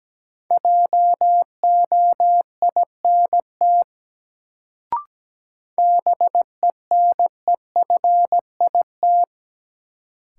Morse code, heard as a keyed tone, about 17 wpm, 700 Hz. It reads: JOINT BENEFIT
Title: Sets of 5 Words Encore - New Word Drill - 17wpm